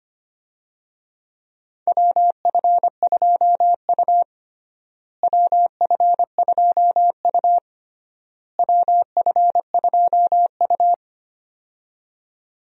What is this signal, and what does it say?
Morse code, heard as a keyed tone, about 25 wpm, 700 Hz. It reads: WF2U WF2U WF2U
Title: USA Call Signs - 4 Characters - Repeated 3x - 25wpm